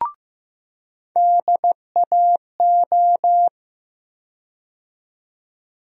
Morse code, heard as a keyed tone, about 15 wpm, 700 Hz. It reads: DAO
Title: USA Call Sign Postfixes - 3 Letters 15wpm